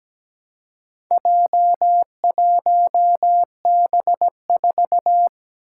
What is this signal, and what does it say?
Morse code, heard as a keyed tone, about 17 wpm, 700 Hz. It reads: J1B4
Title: Four Challenging Letter-Number Combinations 17wpm